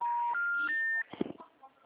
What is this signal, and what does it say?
Special information tone, with speech.